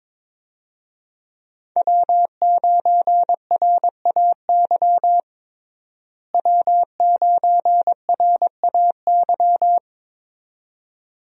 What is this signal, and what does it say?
Morse code, heard as a keyed tone, about 22 wpm, 700 Hz. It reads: W9RAY W9RAY
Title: USA Call Signs - 5 Characters - Repeated 2x - 22wpm